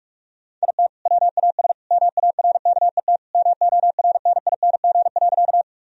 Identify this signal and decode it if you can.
Morse code, heard as a keyed tone, about 46 wpm, 700 Hz. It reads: IT WAS MARKET MORNING.
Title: Sentences from Top 1000 Words - Review Part 3 - 45wpm